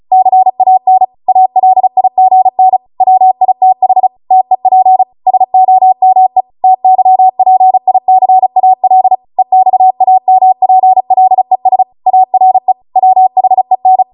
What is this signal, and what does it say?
Morse code, heard as a keyed tone, about 36 wpm, 750 Hz. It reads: CAN ALIGN WITH TEP SOME TYPICAL EXAMPLES ARE WHEN